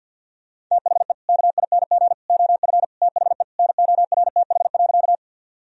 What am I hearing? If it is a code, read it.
Morse code, heard as a keyed tone, about 52 wpm, 700 Hz. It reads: THE KING OF THE NORTH.
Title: Sentences from Top 1000 Words - Review Part 1 - 50wpm